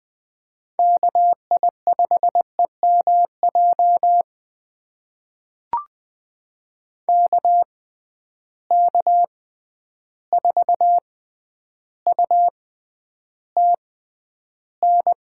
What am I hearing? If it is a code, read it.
Morse code, heard as a keyed tone, about 20 wpm, spaced as for 10 wpm, 700 Hz. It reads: KI5EMJ KK4UTN
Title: USA Call Signs - 6 Characters - 6x Character Spacing - 20wpm